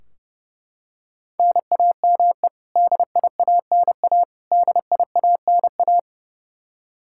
Morse code, heard as a keyed tone, about 30 wpm, 700 Hz. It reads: NAME DIANA DIANA
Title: Practice QSO - Exchange 1 - Beginning - 30wpm